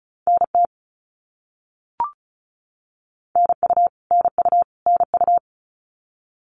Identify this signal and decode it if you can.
Morse code, heard as a keyed tone, about 34 wpm, 700 Hz. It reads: NT NU NU NU
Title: Canadian Provinces and Territories - Repeated 3x - 35wpm